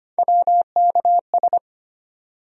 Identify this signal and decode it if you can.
Morse code, heard as a keyed tone, about 25 wpm, 700 Hz. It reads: WKS
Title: USA Call Sign Postfixes - 3 Letters 25wpm